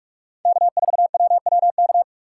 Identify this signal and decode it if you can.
Morse code, heard as a keyed tone, about 42 wpm, 690 Hz. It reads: K4WWK
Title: USA Call Signs - 5 Characters - 4x Character Spacing - 45wpm